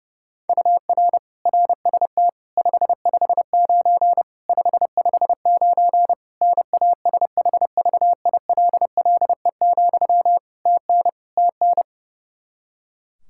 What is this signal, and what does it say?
Morse code, heard as a keyed tone, about 31 wpm, 700 Hz. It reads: UR RST 559 559 NASHVILLE, TN TN
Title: Practice QSO - Exchange 1 - Beginning - 30wpm